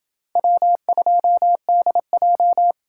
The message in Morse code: W2DJ

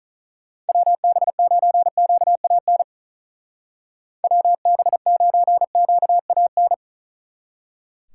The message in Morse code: WB9QAN WB9QAN

